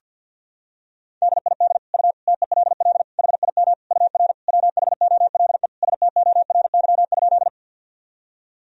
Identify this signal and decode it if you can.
Morse code, heard as a keyed tone, about 52 wpm, 700 Hz. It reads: DID U TELL HIM UR WHOLE STORY?
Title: Sentences from Top 400 Words and Abbreviations 50wpm